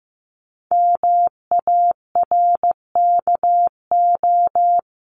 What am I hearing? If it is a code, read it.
Morse code, heard as a keyed tone, about 15 wpm, 700 Hz. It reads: MARKO
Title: CWT Exchange 15wpm